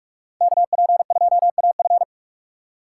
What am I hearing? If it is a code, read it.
Morse code, heard as a keyed tone, about 46 wpm, 700 Hz. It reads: KP2AF
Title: USA Call Signs - 5 Characters - 4x Character Spacing - 45wpm